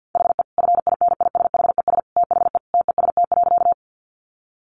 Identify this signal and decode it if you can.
Morse code, heard as a keyed tone, about 50 wpm, 700 Hz. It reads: HE FINISHES THE TEST.
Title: Additional Easy Phrases 50wpm